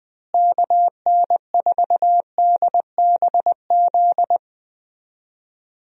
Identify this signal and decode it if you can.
Morse code, heard as a keyed tone, about 20 wpm, 700 Hz. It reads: KN4DBZ
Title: USA Call Signs - 6 Characters - Repeated 2x - 20wpm